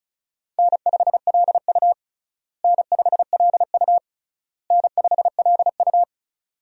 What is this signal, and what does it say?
Morse code, heard as a keyed tone, about 36 wpm, 700 Hz. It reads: N5LU N5LU N5LU
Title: USA Call Signs - 4 Characters - Repeated 3x - 35wpm